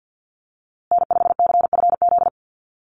Morse code, heard as a keyed tone, about 50 wpm, 700 Hz. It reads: N5CFZ
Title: USA Call Signs Repeated 2x 50wpm